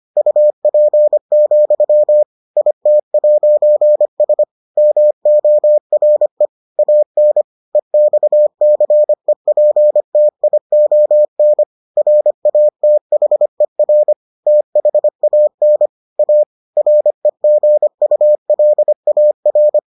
Morse code, heard as a keyed tone, about 25 wpm, 600 Hz. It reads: UP, IT'S MORE AN EXCEPTION RATHER THAN A REGULAR